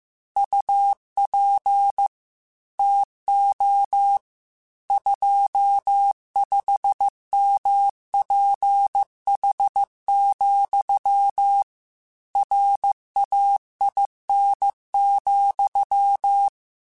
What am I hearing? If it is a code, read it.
Morse code, heard as a keyed tone, about 15 wpm, 800 Hz. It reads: UP TO 25MPH, RAIN,
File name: Weather Sets / Weather Sets-mp3s / Weather 5 5 @13